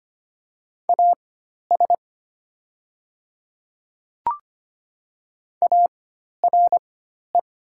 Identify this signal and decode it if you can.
Morse code, heard as a keyed tone, about 25 wpm, spaced as for 12 wpm, 700 Hz. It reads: AS ARE
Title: 4x Character Spacing - Top 100 Words - 25wpm